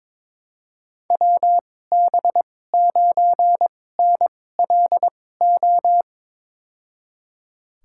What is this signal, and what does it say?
Morse code, heard as a keyed tone, about 21 wpm, spaced as for 16 wpm, 700 Hz. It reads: WB9NLO
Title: USA Call Signs - 6 Characters - 2x Character Spacing - 22wpm